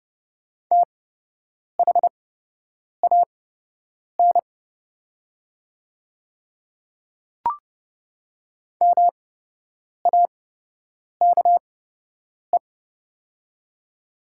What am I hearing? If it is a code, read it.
Morse code, heard as a keyed tone, about 30 wpm, spaced as for 8 wpm, 700 Hz. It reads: THAN MAKE
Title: 8x Character Spacing - Top 100 Words - 30wpm